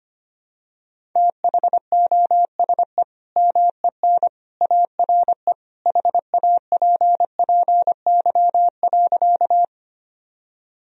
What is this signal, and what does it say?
Morse code, heard as a keyed tone, about 25 wpm, 700 Hz. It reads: THOSE MEN ARE HAPPY.
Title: Sentences from Top 1000 Words - Review Part 1 - 25wpm